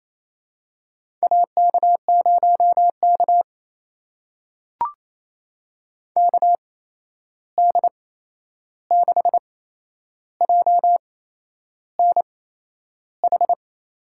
Morse code, heard as a keyed tone, about 28 wpm, spaced as for 11 wpm, 700 Hz. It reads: AK0K KD6JNH
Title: USA Call Signs - 8x Character Spacing - 28wpm